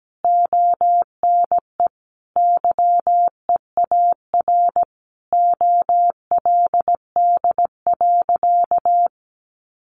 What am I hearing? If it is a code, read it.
Morse code, heard as a keyed tone, about 17 wpm, 700 Hz. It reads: ONE YEAR OLD.
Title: Sentences from Top 1400 Words 17wpm